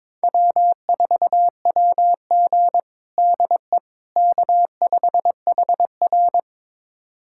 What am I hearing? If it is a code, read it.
Morse code, heard as a keyed tone, about 22 wpm, 700 Hz. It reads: W4WG DE K5HR
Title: USA Callsign DE Callsign - 4 Characters - 22wpm